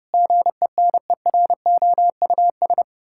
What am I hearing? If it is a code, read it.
Morse code, heard as a keyed tone, about 30 wpm, 700 Hz. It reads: GENEROUS
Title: Pangram Challenge v2 - 30wpm